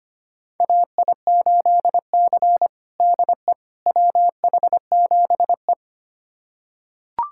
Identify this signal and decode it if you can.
Morse code, heard as a keyed tone, about 25 wpm, 700 Hz. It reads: AI8C DE WH7E
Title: USA Callsign DE Callsign - 4 Characters - 25wpm